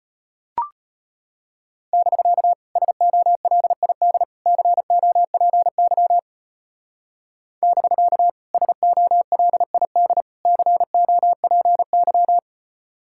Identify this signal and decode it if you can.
Morse code, heard as a keyed tone, about 31 wpm, 700 Hz. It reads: <BK> SOLID COPY <BK> SOLID COPY
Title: Speed-Racing Common QSO Phrases - 25wpm